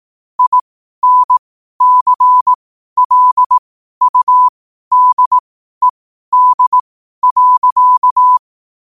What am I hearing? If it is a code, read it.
Morse code, heard as a keyed tone, about 18 wpm, spaced as for 12 wpm, 1000 Hz. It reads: INCLUDED.
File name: qotdfiles / 12wpm